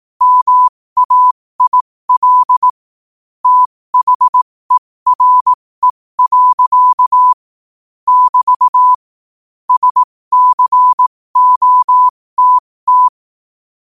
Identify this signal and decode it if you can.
Morse code, heard as a keyed tone, about 18 wpm, spaced as for 15 wpm, 1000 Hz. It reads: MAIL THERE. <BT> SCOTT
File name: qotdfiles / 15wpm